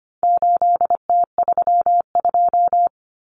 Morse code, heard as a keyed tone, about 25 wpm, 700 Hz. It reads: 8T32
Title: Common Cut Numbers - 4-Digit 25wpm